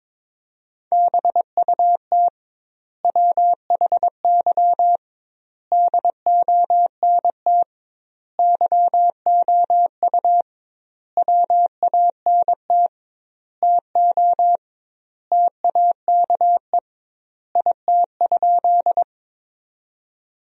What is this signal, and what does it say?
Morse code, heard as a keyed tone, about 22 wpm, 700 Hz. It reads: BUT WHY DONT YOU WANT TO TAKE IT?
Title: 2x Word-Spacing - Sentences from Top 300 Words - 22wpm